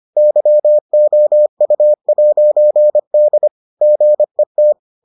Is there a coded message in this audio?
Yes: Morse code from 0.2 to 4.7 s, heard as a keyed tone, about 25 wpm, 600 Hz. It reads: YOU'D GET